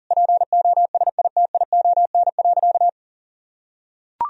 Morse code, heard as a keyed tone, about 40 wpm, 700 Hz. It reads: POSITION.
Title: Sentences from Top 1000 Words - Review Part 2 - 40wpm